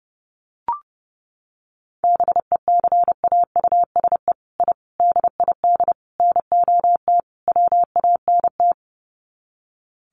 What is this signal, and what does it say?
Morse code, heard as a keyed tone, about 30 wpm, 700 Hz. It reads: BECAUSE I DID NOT WANT
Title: Sets of 5 Words Encore2 30wpm